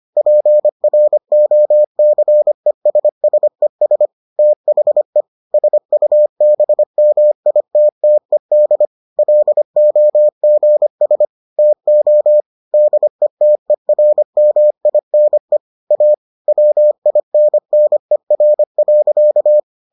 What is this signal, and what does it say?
Morse code, heard as a keyed tone, about 25 wpm, 600 Hz. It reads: PROCESSES THE SUBMITTED LOGS TO DETERMINE A WINNER.